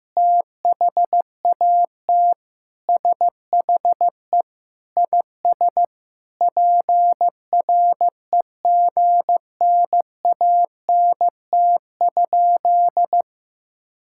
Morse code, heard as a keyed tone, about 16 wpm, 700 Hz. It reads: THAT SHE IS PREGNANT?
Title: Sentences from Top 1000 Words - Review Part 6 - 15wpm